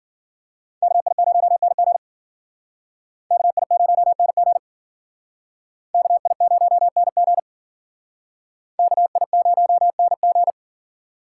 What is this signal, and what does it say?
Morse code, heard as a keyed tone, about 50 wpm, 700 Hz. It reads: KI0NG KI0NG KI0NG KI0NG
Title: Speed Racing - USA 2x2 Call Signs 40wpm